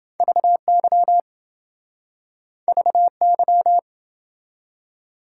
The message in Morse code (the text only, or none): VY VY